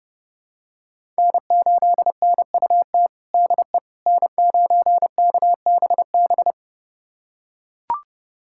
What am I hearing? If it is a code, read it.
Morse code, heard as a keyed tone, about 32 wpm, 700 Hz. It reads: N8NUT DE N9KBB
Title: USA Callsign DE Callsign - 5 Characters - 30wpm